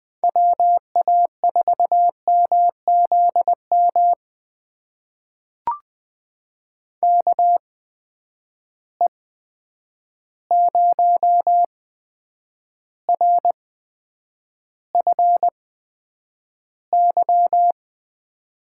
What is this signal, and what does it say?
Morse code, heard as a keyed tone, about 20 wpm, spaced as for 9 wpm, 700 Hz. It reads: WA4MZM KE0RFY
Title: USA Call Signs - 6 Characters - 8x Character Spacing - 20wpm